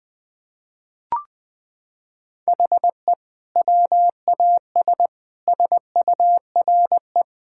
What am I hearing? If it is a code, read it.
Morse code, heard as a keyed tone, about 20 wpm, 700 Hz. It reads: HE WAS SURE
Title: Sentences from Top 1000 Words - Review Part 6 - 20wpm